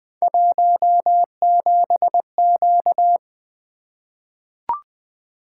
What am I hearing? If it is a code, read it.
Morse code, heard as a keyed tone, about 20 wpm, 700 Hz. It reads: 17Q
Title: Three Letter-Number Combinations - 20wpm